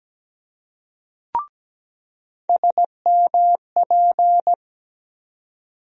Morse code, heard as a keyed tone, about 17 wpm, 700 Hz. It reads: SMP
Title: USA Call Sign Postfixes - 3 Letters 17wpm